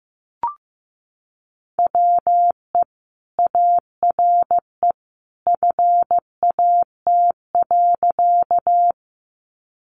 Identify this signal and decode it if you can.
Morse code, heard as a keyed tone, about 15 wpm, 700 Hz. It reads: WE ARE FAT.
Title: Sentences from Top 600 Words 15wpm